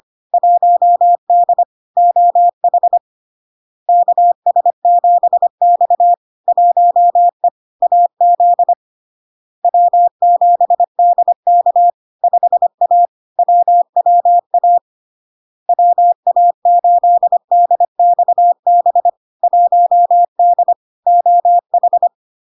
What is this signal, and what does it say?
Morse code, heard as a keyed tone, about 25 wpm, 700 Hz. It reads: 1D OH KS7X 1E AZ W7DK 5A WWA WA8DXB 1D OH
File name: LARK-FD-2022-Log-in-CW-25wpm-700-Hz